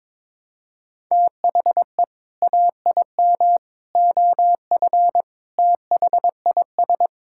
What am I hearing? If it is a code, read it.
Morse code, heard as a keyed tone, about 22 wpm, 700 Hz. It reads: THE AIM OF THIS
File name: practice / 22wpm / Sets of 7 Words - 22wpm